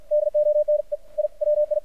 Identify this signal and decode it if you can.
Morse code, heard as a keyed tone, about 53 wpm, 600 Hz. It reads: DONE I WI
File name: archived / CRSnet